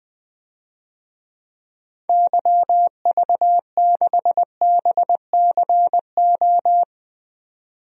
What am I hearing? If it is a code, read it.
Morse code, heard as a keyed tone, about 20 wpm, 700 Hz. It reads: YV6BCO
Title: DXCC-100 Call Signs - 6 Characters - 20wpm